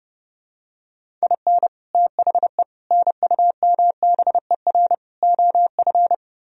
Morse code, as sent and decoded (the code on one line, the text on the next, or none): .. -. / - .... / -. ..- -- -... .-. / --- ..-.
IN THE NUMBER OF